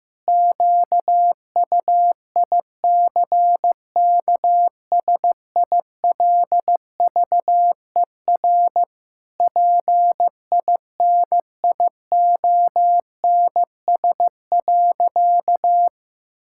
Morse code, heard as a keyed tone, about 16 wpm, 700 Hz. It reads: QUICKSILVER PINIONS.
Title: Pangram Challenge v2 - 15wpm